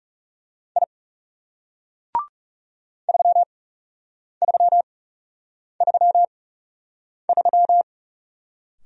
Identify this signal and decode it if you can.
Morse code, heard as a keyed tone, about 37 wpm, spaced as for 8 wpm, 700 Hz. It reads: I 3333
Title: Speed-Racing Single Letter-Number 30wpm